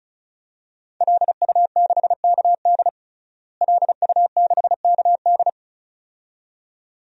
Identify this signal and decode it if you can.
Morse code, heard as a keyed tone, about 34 wpm, 700 Hz. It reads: LU6KD LU6KD
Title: DXCC-100 Call Signs - 5 Characters - Repeated 2x - 35wpm